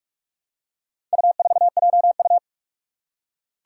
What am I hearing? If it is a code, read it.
Morse code, heard as a keyed tone, about 43 wpm, 700 Hz. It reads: U4JU
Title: Four Challenging Letter-Number Combinations 45wpm